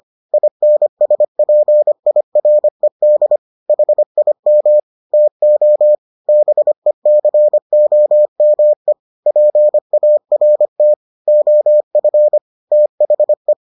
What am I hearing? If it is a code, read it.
Morse code, heard as a keyed tone, about 25 wpm, 600 Hz. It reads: INSPIRED HIM TO BECOME PART OF THE